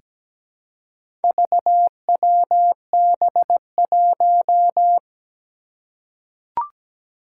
Morse code, heard as a keyed tone, about 17 wpm, 700 Hz. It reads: VWB1
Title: Four Challenging Letter-Number Combinations 17wpm